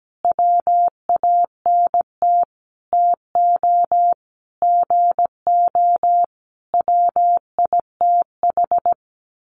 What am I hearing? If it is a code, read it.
Morse code, heard as a keyed tone, about 17 wpm, 700 Hz. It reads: WANT TO GO WITH
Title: Sentences from Top 1300 Words 17wpm